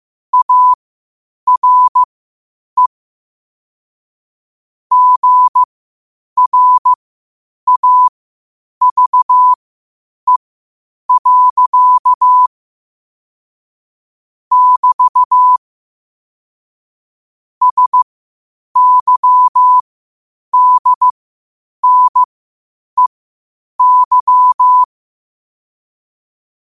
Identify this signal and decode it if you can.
Morse code, heard as a keyed tone, about 15 wpm, spaced as for 8 wpm, 1000 Hz. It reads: ARE GRAVE. <BT> SYDNEY
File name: qotdfiles / 8wpm